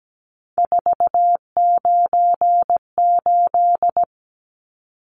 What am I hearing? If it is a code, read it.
Morse code, heard as a keyed tone, about 17 wpm, 700 Hz. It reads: 498
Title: Numbers - 3-Digit 17wpm